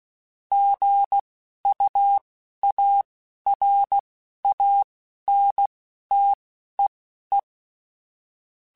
Morse code, heard as a keyed tone, about 16 wpm, spaced as for 11 wpm, 800 Hz. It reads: GUARANTEE
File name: Quotes / Quotes-mp3s / Airtight @10